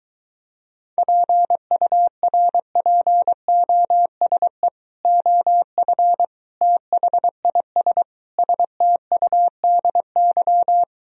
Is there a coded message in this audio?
Yes: Morse code from 1.0 to 10.8 s, heard as a keyed tone, about 23 wpm, 700 Hz. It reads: PURPOSE OF THIS STUDY